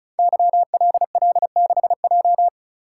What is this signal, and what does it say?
Morse code, heard as a keyed tone, about 36 wpm, 690 Hz. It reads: YLL6J
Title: Code Groups - 5 Letter:Numbers - 35wpm